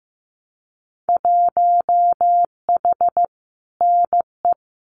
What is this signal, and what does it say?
Morse code, heard as a keyed tone, about 15 wpm, 700 Hz. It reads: 1H NE
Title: Winter Field Day Exchanges 15wpm